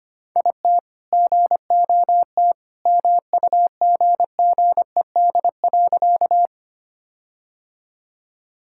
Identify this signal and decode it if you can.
Morse code, heard as a keyed tone, about 25 wpm, 700 Hz. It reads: IT GOT MUGGED.
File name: practice / 25wpm / All-Age Appropriate Jokes - 25wpm